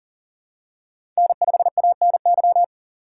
Morse code, heard as a keyed tone, about 40 wpm, 700 Hz. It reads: N5ANY
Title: USA Call Signs - 5 Characters - 2x Character Spacing - 40wpm